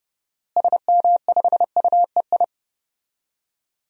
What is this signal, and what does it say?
Morse code, heard as a keyed tone, about 30 wpm, 700 Hz. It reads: SM5UEI
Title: DXCC-100 Call Signs - 6 Characters - Repeated 2x - 30wpm